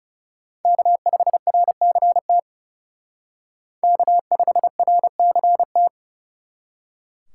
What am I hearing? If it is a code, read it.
Morse code, heard as a keyed tone, about 37 wpm, 700 Hz. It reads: K5RCT K5RCT
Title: Speed Racing - USA Call Signs 30wpm